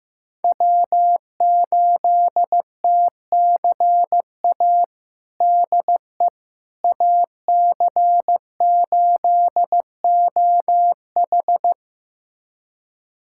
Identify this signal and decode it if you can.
Morse code, heard as a keyed tone, about 15 wpm, 700 Hz. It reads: W8TCA DE AC8OH
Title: USA Callsign DE Callsign - 5 Characters - 15wpm